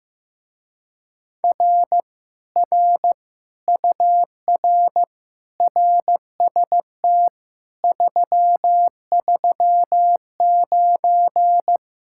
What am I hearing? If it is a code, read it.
Morse code, heard as a keyed tone, about 15 wpm, 700 Hz. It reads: R R UR RST 339
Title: Speed-Racing Common QSO Phrases Encore - 15wpm